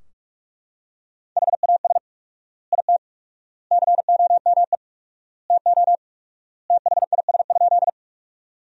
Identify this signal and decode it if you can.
Morse code, heard as a keyed tone, about 42 wpm, 700 Hz. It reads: HAS IT COME TO THIS?
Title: 4x Word-Spacing - Sentences from Top 100 Words - 45wpm